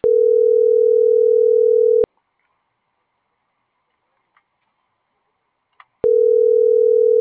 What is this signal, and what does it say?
Ringback tone (North America), 440 + 480 Hz.